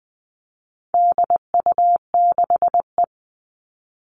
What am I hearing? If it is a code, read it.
Morse code, heard as a keyed tone, about 20 wpm, 700 Hz. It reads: DU6E
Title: All Cut Numbers - 4-Digit 20wpm